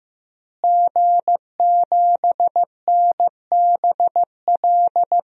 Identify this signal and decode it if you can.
Morse code, heard as a keyed tone, about 15 wpm, 700 Hz. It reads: G7NBL
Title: DXCC-100 Call Signs - 5 Characters - Repeated 2x - 15wpm